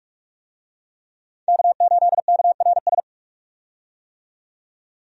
Morse code, heard as a keyed tone, about 47 wpm, 700 Hz. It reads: K8KRS
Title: USA Call Signs - 5 Characters - 45wpm